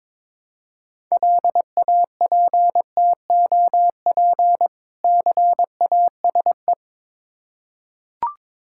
Morse code, heard as a keyed tone, about 22 wpm, 690 Hz. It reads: LAPTOP CASE
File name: Difficult Words with Hints - 22wpm